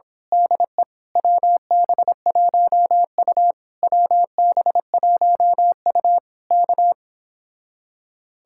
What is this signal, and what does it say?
Morse code, heard as a keyed tone, about 26 wpm, 700 Hz. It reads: DE WB1U WB1U K
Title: Speed-Racing Common QSO Phrases - 22wpm